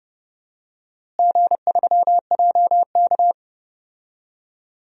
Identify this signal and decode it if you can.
Morse code, heard as a keyed tone, about 30 wpm, 700 Hz. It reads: G3JK
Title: DXCC-100 Call Signs - 4 Characters - Repeated 2x - 30wpm